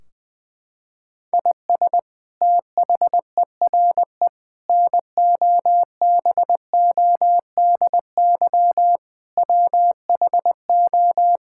Morse code, heard as a keyed tone, about 20 wpm, 700 Hz. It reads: IS THERE NOBODY WHO
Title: Sentences from Top 1600 Words 20wpm